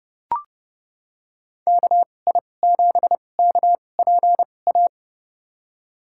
Morse code, heard as a keyed tone, about 32 wpm, spaced as for 22 wpm, 700 Hz. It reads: KI7KPA
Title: USA Call Signs - 6 Characters - 2x Character Spacing - 30wpm